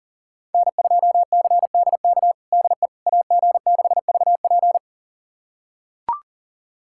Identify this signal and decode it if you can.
Morse code, heard as a keyed tone, about 37 wpm, 690 Hz. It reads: N2CDK DE AG6VP
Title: USA Callsign DE Callsign - 5 Characters - 40wpm